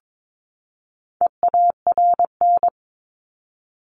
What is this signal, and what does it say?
Morse code, heard as a keyed tone, about 22 wpm, 700 Hz. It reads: EARN
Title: Top 1251-1500 Words 22wpm